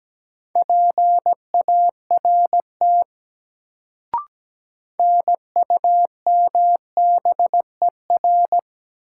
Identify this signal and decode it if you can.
Morse code, heard as a keyed tone, about 17 wpm, 700 Hz. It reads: PART NUMBER
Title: Sets of 2 Words - New Word Drill - 17wpm